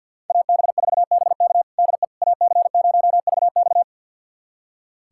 Morse code, heard as a keyed tone, about 46 wpm, 700 Hz. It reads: AB4BK DE AC0VX
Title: USA Callsign DE Callsign - 5 Characters - 50wpm